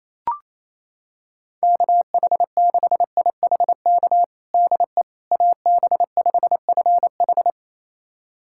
Morse code, heard as a keyed tone, about 28 wpm, 700 Hz. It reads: KH6IHK DE AB5FH
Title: USA Callsign DE Callsign 28wpm